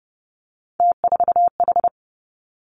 Morse code, heard as a keyed tone, about 30 wpm, 700 Hz. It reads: T4H
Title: Three Challenging Letter-Number Combinations 30wpm